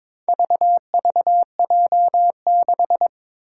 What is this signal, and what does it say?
Morse code, heard as a keyed tone, about 22 wpm, 700 Hz. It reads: VVJ6
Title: Four Challenging Letter-Number Combinations 22wpm